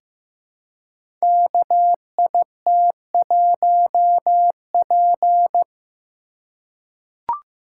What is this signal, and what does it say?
Morse code, heard as a keyed tone, about 15 wpm, 700 Hz. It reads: KIT1P
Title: Code Groups - 5 Letter:Numbers - 15wpm